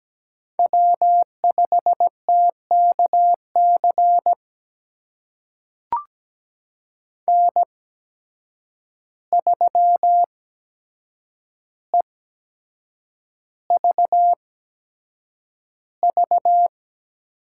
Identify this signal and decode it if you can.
Morse code, heard as a keyed tone, about 17 wpm, spaced as for 7 wpm, 700 Hz. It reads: W5TKC N3EVV
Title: USA Call Signs - 5 Characters - 8x Character Spacing - 17wpm